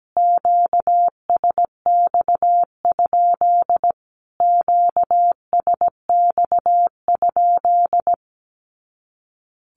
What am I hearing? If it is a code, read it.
Morse code, heard as a keyed tone, about 17 wpm, 700 Hz. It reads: QSX? QSX?
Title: Q-Codes Repeated 3x 17wpm